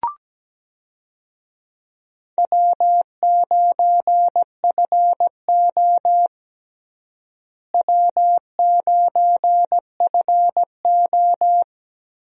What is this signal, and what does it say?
Morse code, heard as a keyed tone, about 17 wpm, 700 Hz. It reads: W9FO W9FO
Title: USA Call Signs - 4 Characters - Repeated 3x - 17wpm